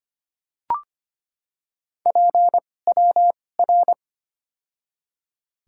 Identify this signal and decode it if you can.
Morse code, heard as a keyed tone, about 25 wpm, spaced as for 18 wpm, 700 Hz. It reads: PWR
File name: Top 90 QSO Elements - 2x Character Spacing - 25wpm